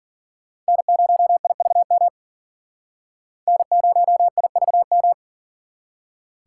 Morse code, heard as a keyed tone, about 51 wpm, 700 Hz. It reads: N0IVM N0IVM